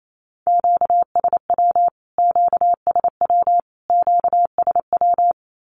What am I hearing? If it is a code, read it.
Morse code, heard as a keyed tone, about 28 wpm, 700 Hz. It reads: QSW QSW QSW
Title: Q-Codes Repeated 3x 28wpm